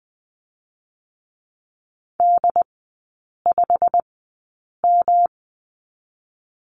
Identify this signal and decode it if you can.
Morse code, heard as a keyed tone, about 20 wpm, 700 Hz. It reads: D 5 M
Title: 145 Lesson 24e - Letter y - Three Characters 20wpm